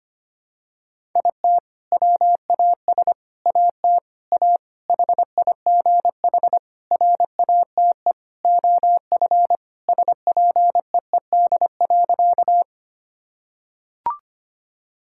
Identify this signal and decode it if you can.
Morse code, heard as a keyed tone, about 25 wpm, 700 Hz. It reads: IT WAS AT A HIGH RATE OF SPEED.
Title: Sentences from Top 1000 Words - Review Part 5 - 25wpm